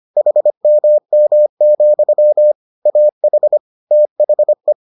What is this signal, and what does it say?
Morse code, heard as a keyed tone, about 25 wpm, 600 Hz. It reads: HMM, AH THE